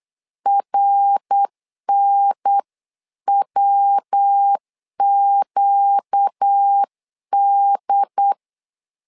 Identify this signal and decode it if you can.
Morse code, heard as a keyed tone, about 8 wpm, 800 Hz. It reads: RNWQD
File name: LEZIONI / 4 Q1S 8wpm 2710